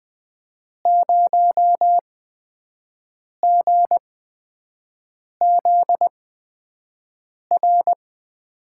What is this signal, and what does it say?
Morse code, heard as a keyed tone, about 20 wpm, spaced as for 5 wpm, 700 Hz. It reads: 0GZR
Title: USA Call Signs - 8x Character Spacing - 20wpm